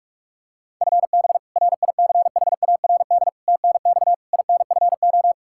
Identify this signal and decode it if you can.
Morse code, heard as a keyed tone, about 41 wpm, 700 Hz. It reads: FB RICHARD TNX INFO
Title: Practice QSO - Exchange 3 - More Info - 45wpm